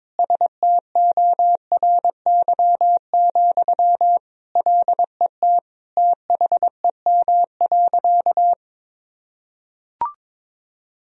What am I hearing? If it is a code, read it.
Morse code, heard as a keyed tone, about 22 wpm, 700 Hz. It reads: STORY, LET THEM.
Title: Sentences from Top 1000 Words - Review Part 2 - 22wpm